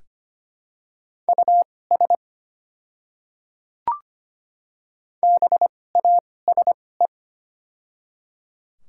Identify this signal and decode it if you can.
Morse code, heard as a keyed tone, about 25 wpm, spaced as for 18 wpm, 700 Hz. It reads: US BASE
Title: 2x Character Spacing - Top 300 Words - 25wpm